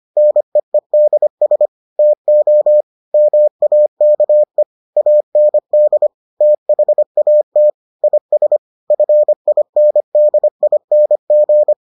Morse code, heard as a keyed tone, about 25 wpm, 600 Hz. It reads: NEEDS TO MAKE AND THAT IS FINDING